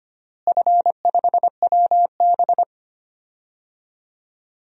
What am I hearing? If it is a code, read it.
Morse code, heard as a keyed tone, about 25 wpm, 700 Hz. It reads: F5WB
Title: DXCC-100 Call Signs - 4 Characters - Repeated 3x - 25wpm